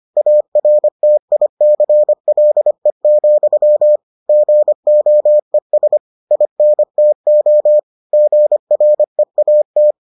Morse code, heard as a keyed tone, about 25 wpm, 600 Hz. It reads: ARTICLE, GOES INTO GREAT